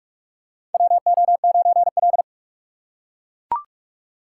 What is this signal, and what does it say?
Morse code, heard as a keyed tone, about 41 wpm, 690 Hz. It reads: WO9L
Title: USA Call Signs - 4 Characters - 2x Character Spacing - 45wpm